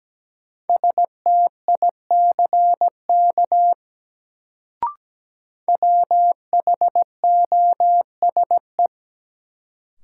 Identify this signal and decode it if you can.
Morse code, heard as a keyed tone, about 17 wpm, 700 Hz. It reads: STICK WHOSE
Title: Sentences from Top 1200 Words - New Word Drill - 17wpm